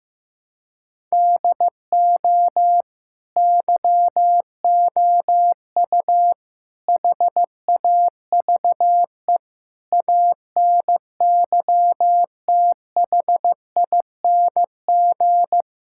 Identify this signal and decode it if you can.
Morse code, heard as a keyed tone, about 15 wpm, 700 Hz. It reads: DO YOU HAVE ANYTHING
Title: Sentences from Top 1000 Words - Review Part 5 - 15wpm